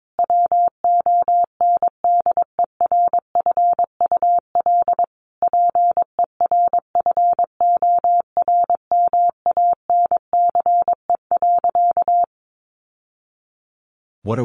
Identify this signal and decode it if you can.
Morse code, heard as a keyed tone, about 22 wpm, 700 Hz. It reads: WONDERFUL PERFORMANCE.
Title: Sentences from Top 1100 Words 22wpm